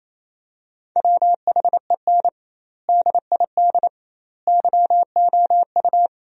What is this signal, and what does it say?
Morse code, heard as a keyed tone, about 28 wpm, 700 Hz. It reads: WHEN DID YOU
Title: 2x Word-Spacing - Sentences from Top 300 Words - 28wpm